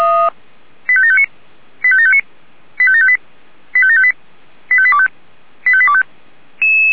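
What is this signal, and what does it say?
Five-tone selective call (ZVEI), heard as quick runs of five tones, tones of about 70 ms. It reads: 85759 85759 85758 85758 85825 85825
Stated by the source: ZVEI five-tone call 85730 received with an SDR.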